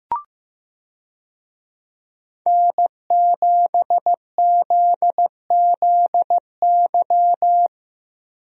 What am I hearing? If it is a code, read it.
Morse code, heard as a keyed tone, about 15 wpm, 700 Hz. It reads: N7ZZY
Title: USA Call Signs - 5 Characters - Repeated 3x - 15wpm